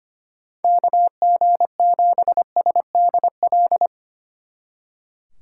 Morse code, heard as a keyed tone, about 26 wpm, 700 Hz. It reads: KG7SDL
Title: Speed Racing - USA Call Signs 25wpm